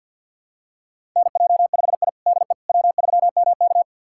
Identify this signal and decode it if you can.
Morse code, heard as a keyed tone, about 52 wpm, 700 Hz. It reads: NJ5I DE W3MK
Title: USA Callsign DE Callsign - 4 Characters - 50wpm